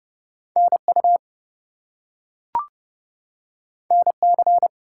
Morse code, heard as a keyed tone, about 30 wpm, 700 Hz. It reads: NU NC
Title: US States and Canadian Provinces 30wpm